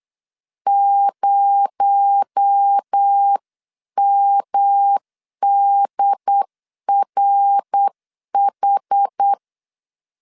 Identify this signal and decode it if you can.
Morse code, heard as a keyed tone, about 8 wpm, 800 Hz. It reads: 0MDRH
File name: CW / DET / 1 6 1500C S C 8wpm 5219